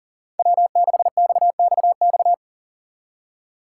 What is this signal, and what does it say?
Morse code, heard as a keyed tone, about 40 wpm, 700 Hz. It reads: W6XXX
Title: USA Call Signs - 5 Characters - 4x Character Spacing - 40wpm